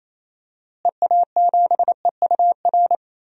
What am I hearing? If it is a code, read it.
Morse code, heard as a keyed tone, about 28 wpm, 700 Hz. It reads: EA7EUR
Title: DXCC-100 Call Signs - 6 Characters - Repeated 2x - 28wpm